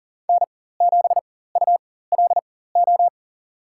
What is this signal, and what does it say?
Morse code, heard as a keyed tone, about 40 wpm, spaced as for 19 wpm, 700 Hz. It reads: N7ULO
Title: USA Call Signs - 5 Characters - 4x Character Spacing - 40wpm